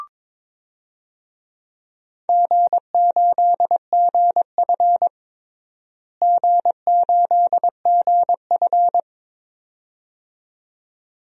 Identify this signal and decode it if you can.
Morse code, heard as a keyed tone, about 22 wpm, 700 Hz. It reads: G8GF G8GF
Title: DXCC-100 Call Signs - 4 Characters - Repeated 2x - 22wpm